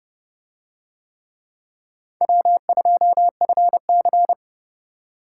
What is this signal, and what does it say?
Morse code, heard as a keyed tone, about 30 wpm, 700 Hz. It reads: W2FC